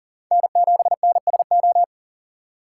Morse code, heard as a keyed tone, about 42 wpm, 700 Hz. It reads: N7NSO
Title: USA Call Signs - 5 Characters - 6x Character Spacing - 40wpm